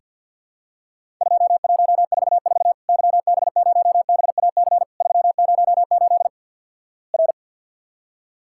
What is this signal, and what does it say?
Morse code, heard as a keyed tone, about 47 wpm, 700 Hz. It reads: 2144 YB0BAC 308 R
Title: Worked All Europe DX Contest - QTC Exchange - 50wpm